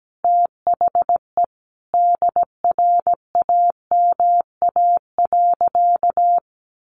Morse code, heard as a keyed tone, about 17 wpm, 700 Hz. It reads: THE DRAMA.
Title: Sentences from Top 1500 Words 17wpm